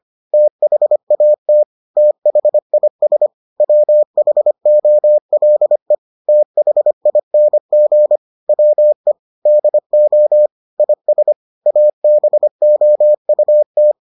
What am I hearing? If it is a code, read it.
Morse code, heard as a keyed tone, about 25 wpm, 600 Hz. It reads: THAT THIS WHOLE THING WE DO IS ABOUT